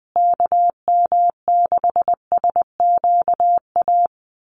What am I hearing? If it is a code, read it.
Morse code, heard as a keyed tone, about 20 wpm, 700 Hz. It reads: KM6SQA
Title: USA Call Signs Repeated 2x 20wpm